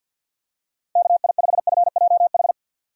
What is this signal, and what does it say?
Morse code, heard as a keyed tone, about 52 wpm, 700 Hz. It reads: KI5FJH
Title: USA Call Signs - 6 Characters - 50wpm